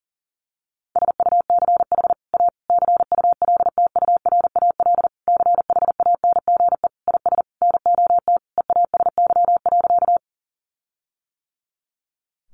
Morse code, heard as a keyed tone, about 40 wpm, 700 Hz. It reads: SUCH A CULTURAL CHANGE IS NOT EASY.
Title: Sentences from Top 1500 Words 40wpm